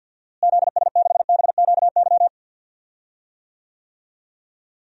Morse code, heard as a keyed tone, about 48 wpm, 700 Hz. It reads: ZS6BQY
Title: DXCC-100 Call Signs - 6 Characters - Repeated 3x - 50wpm